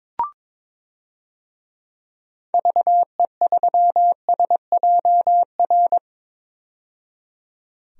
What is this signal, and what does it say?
Morse code, heard as a keyed tone, about 22 wpm, 700 Hz. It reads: VE3SJR
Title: DXCC-100 Call Signs - 6 Characters - 22wpm